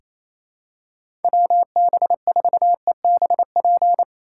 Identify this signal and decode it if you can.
Morse code, heard as a keyed tone, about 28 wpm, 700 Hz. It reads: WB4EBP
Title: USA Call Signs - 6 Characters - 3x Character Spacing - 28wpm